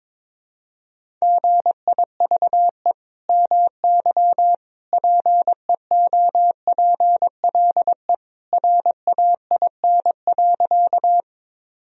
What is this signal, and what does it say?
Morse code, heard as a keyed tone, about 22 wpm, 700 Hz. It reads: GIVE MY PEOPLE RAIN.
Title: Sentences from Top 1000 Words - Review Part 1 - 22wpm